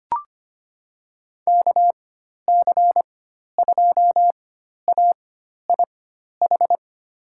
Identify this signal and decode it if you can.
Morse code, heard as a keyed tone, about 25 wpm, spaced as for 12 wpm, 700 Hz. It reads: KC2AIH
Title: USA Call Signs - 4x Character Spacing - 25wpm